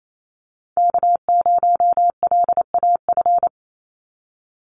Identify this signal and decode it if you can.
Morse code, heard as a keyed tone, about 28 wpm, 700 Hz. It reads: K0LAF
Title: NA SOTA Chaser Callsigns 28wpm